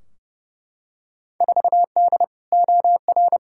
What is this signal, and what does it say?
Morse code, heard as a keyed tone, about 30 wpm, 700 Hz. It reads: 4D OR
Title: Field Day Exchanges Encore - 30wpm